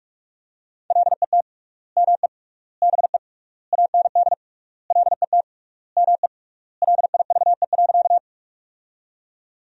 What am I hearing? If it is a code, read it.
Morse code, heard as a keyed tone, about 46 wpm, 700 Hz. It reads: LET ME BE AND LET ME LIVE.
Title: 3x Word-Spacing - Sentences from Top 300 Words - 45wpm